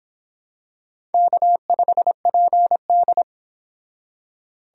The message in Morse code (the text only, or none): K5PD